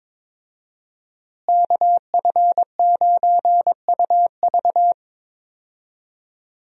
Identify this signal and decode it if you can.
Morse code, heard as a keyed tone, about 22 wpm, 700 Hz. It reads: KF9UV
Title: USA Call Signs - 5 Characters - Repeated 3x - 22wpm